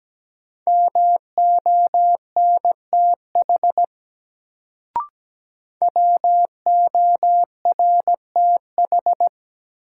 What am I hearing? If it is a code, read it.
Morse code, heard as a keyed tone, about 17 wpm, 700 Hz. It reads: MONTH WORTH
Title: Sentences from Top 800 Words - New Word Drill - 17wpm